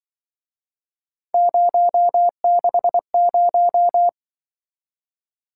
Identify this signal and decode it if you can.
Morse code, heard as a keyed tone, about 24 wpm, 700 Hz. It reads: 060
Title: Speed-Racing - Three Letter-Number Combinations - 20wpm